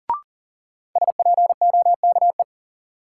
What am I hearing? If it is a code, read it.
Morse code, heard as a keyed tone, about 40 wpm, 700 Hz. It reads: SPOKE
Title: Sentences from Top 1200 Words - New Word Drill Alt - 40wpm